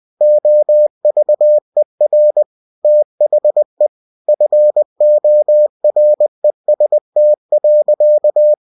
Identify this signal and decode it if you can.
Morse code, heard as a keyed tone, about 20 wpm, 600 Hz. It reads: OVER THE FOREST.
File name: winnie-20wpm